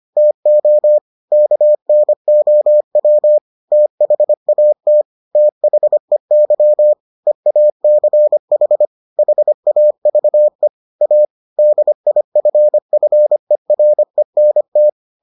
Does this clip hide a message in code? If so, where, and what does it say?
Yes: Morse code from 0.2 to 14.9 s, heard as a keyed tone, about 25 wpm, 600 Hz. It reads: TO KNOW THAT THEY EACH HAVE A DIFFERENT